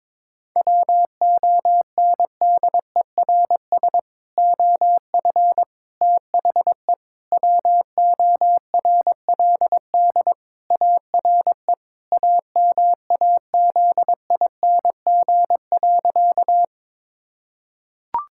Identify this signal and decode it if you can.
Morse code, heard as a keyed tone, about 22 wpm, 700 Hz. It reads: WONDERS OF THE WORLD ARE AMAZING.
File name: practice / 22wpm / Sentences with 3-Word Phrases - 22wpm